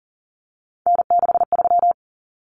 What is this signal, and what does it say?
Morse code, heard as a keyed tone, about 40 wpm, 700 Hz. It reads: N63
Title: Common Cut Numbers - 3-Digit 40wpm